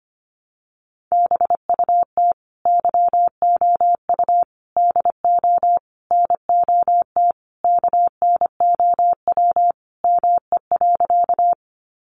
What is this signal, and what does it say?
Morse code, heard as a keyed tone, about 25 wpm, 700 Hz. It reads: BUT YOU DO NOT KNOW ME.
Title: Sentences from Top 600 Words 25wpm